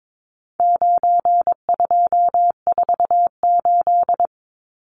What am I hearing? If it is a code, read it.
Morse code, heard as a keyed tone, about 22 wpm, 700 Hz. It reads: 9248
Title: Numbers - 4-Digit 22wpm